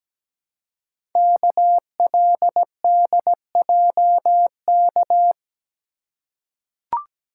Morse code, heard as a keyed tone, about 17 wpm, 700 Hz. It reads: KLDJK
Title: Code Groups - 5 Letters - 17wpm